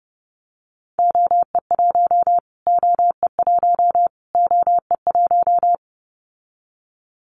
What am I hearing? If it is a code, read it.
Morse code, heard as a keyed tone, about 30 wpm, 700 Hz. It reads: OE1 OE1 OE1
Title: Callsign Prefixes for Top 25 DXCC - Repeated 3x - 30wpm